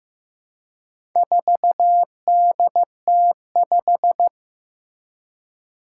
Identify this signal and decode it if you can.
Morse code, heard as a keyed tone, about 15 wpm, 700 Hz. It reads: 4DT5
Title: Four Challenging Letter-Number Combinations 15wpm